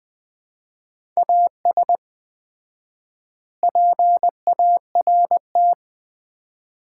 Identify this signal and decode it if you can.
Morse code, heard as a keyed tone, about 20 wpm, 700 Hz. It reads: AS PART
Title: 4x Word-Spacing - Sets of 3 Words - 20wpm